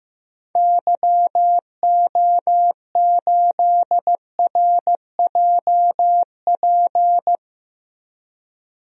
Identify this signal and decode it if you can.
Morse code, heard as a keyed tone, about 16 wpm, 700 Hz. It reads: YO8RJP
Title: DXCC-100 Call Signs - 6 Characters - 15wpm